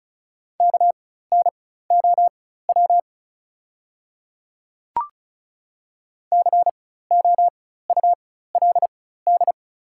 Morse code, heard as a keyed tone, about 36 wpm, spaced as for 16 wpm, 700 Hz. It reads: KNOW COULD
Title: 4x Character Spacing - Top 100 Words - 35wpm